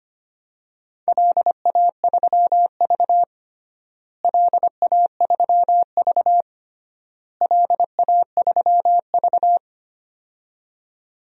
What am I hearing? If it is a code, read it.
Morse code, heard as a keyed tone, about 25 wpm, 700 Hz. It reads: LA3V LA3V LA3V
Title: DXCC-100 Call Signs - 4 Characters - Repeated 3x - 25wpm